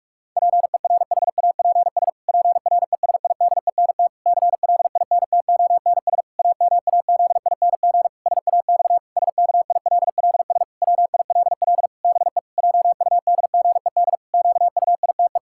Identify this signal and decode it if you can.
Morse code, heard as a keyed tone, about 46 wpm, 700 Hz. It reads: PERHAPS PRESIDENT CLINTONS AMAZING SAX SKILLS WILL BE JUDGED QUITE
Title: Pangram Challenge v2 - 45wpm